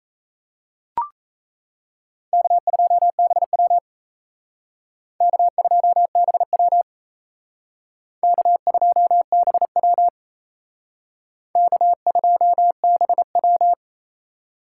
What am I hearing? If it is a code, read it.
Morse code, heard as a keyed tone, about 35 wpm, 700 Hz. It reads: K2BW K2BW K2BW K2BW
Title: Speed Racing - USA Call Signs 28wpm